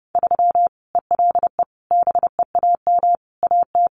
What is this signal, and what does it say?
Morse code, heard as a keyed tone, about 30 wpm, 700 Hz. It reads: 3 ELE BEAM AT
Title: Common QSO Phrases 30wpm